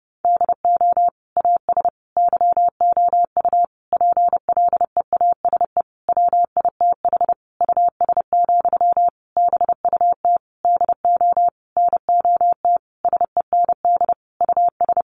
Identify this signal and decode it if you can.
Morse code, heard as a keyed tone, about 30 wpm, 700 Hz. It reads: DO AS YOU PLEASE WITH US, BUT DO NOT SEND US
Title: Sentences from Top 1500 Words 30wpm